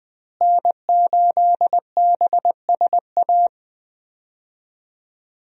Morse code, heard as a keyed tone, about 20 wpm, 700 Hz. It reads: N8BSA
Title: USA Call Signs - 5 Characters - Repeated 3x - 20wpm